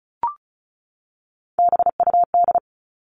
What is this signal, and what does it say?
Morse code, heard as a keyed tone, about 34 wpm, 700 Hz. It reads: BUD
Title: All Cut Numbers - 3-Digit 35wpm